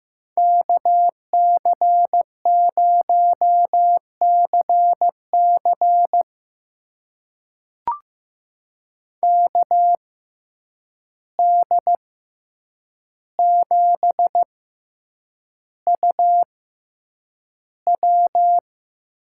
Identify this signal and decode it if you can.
Morse code, heard as a keyed tone, about 15 wpm, spaced as for 8 wpm, 700 Hz. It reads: KC0CC KD7UW
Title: USA Call Signs - 5 Characters - 6x Character Spacing - 15wpm